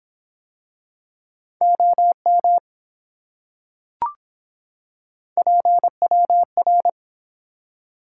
Morse code, heard as a keyed tone, about 24 wpm, 700 Hz. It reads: OM PWR